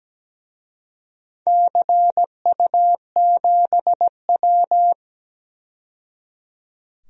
Morse code, heard as a keyed tone, about 17 wpm, 700 Hz. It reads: CU7W
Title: DXCC-100 Call Signs - 4 Characters - 17wpm